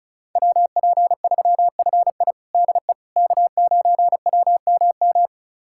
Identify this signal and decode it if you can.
Morse code, heard as a keyed tone, about 34 wpm, 700 Hz. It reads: WP3FI DE K9WMM
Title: USA Callsign DE Callsign - 5 Characters - 35wpm